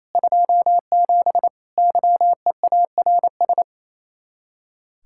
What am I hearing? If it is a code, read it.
Morse code, heard as a keyed tone, about 36 wpm, 700 Hz. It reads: 27 YEARS